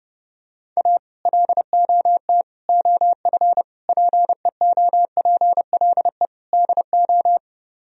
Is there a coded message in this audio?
Yes: Morse code from 0.8 to 7.4 s, heard as a keyed tone, about 30 wpm, 700 Hz. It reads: A LOT OF PEOPLE DO